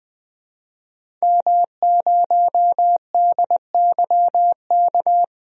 Morse code, heard as a keyed tone, about 20 wpm, 700 Hz. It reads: M0DYK
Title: DXCC-100 Call Signs - 5 Characters - 20wpm